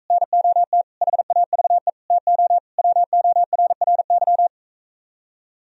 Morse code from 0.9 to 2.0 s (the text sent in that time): HAVE